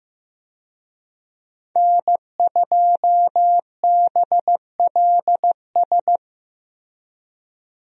Morse code, heard as a keyed tone, about 15 wpm, 700 Hz. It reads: N2BLS
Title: USA Call Signs - 5 Characters - Repeated 3x - 15wpm